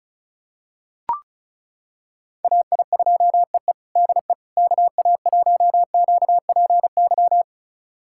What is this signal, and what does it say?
Morse code, heard as a keyed tone, about 36 wpm, 700 Hz. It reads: AI2EE DE KA1QPY
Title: USA Callsign DE Callsign 35wpm